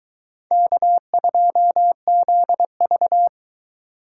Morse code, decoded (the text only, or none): K2ZV